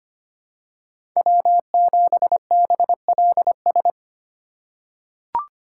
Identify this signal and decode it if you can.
Morse code, heard as a keyed tone, about 25 wpm, 700 Hz. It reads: W7BLS
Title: USA Call Signs - 5 Characters - 2x Character Spacing - 25wpm